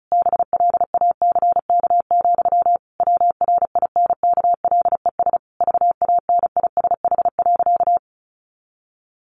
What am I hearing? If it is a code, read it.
Morse code, heard as a keyed tone, about 34 wpm, 700 Hz. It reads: BLACK, WRINKLES VANISH.
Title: A Sound Of Thunder 35wpm